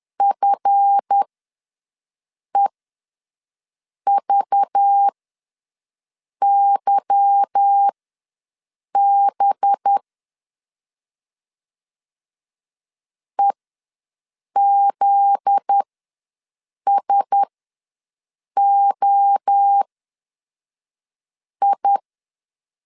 Morse code, heard as a keyed tone, about 10 wpm, spaced as for 5 wpm, 800 Hz. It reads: FEVYB EZSOI
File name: LEZIONI / 8 YJ45 / 8 Y ws 10wpm 3952